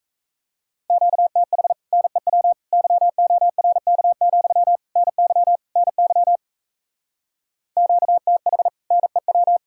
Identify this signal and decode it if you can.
Morse code, heard as a keyed tone, about 37 wpm, 700 Hz. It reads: QTH NEW YORK, NY NY QTH NEW